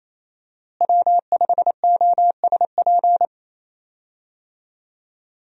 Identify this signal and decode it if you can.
Morse code, heard as a keyed tone, about 28 wpm, 700 Hz. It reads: W5OSP
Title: USA Call Signs - 5 Characters - Repeated 3x - 28wpm